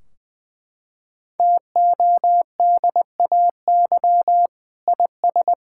Morse code, heard as a keyed tone, about 20 wpm, 700 Hz. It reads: TODAY IS
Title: Sentences from Top 1600 Words 20wpm